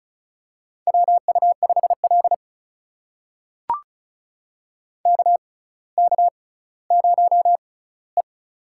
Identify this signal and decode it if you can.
Morse code, heard as a keyed tone, about 36 wpm, spaced as for 18 wpm, 700 Hz. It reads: WU5L KK0E
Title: USA Call Signs - 4 Characters - 6x Character Spacing - 35wpm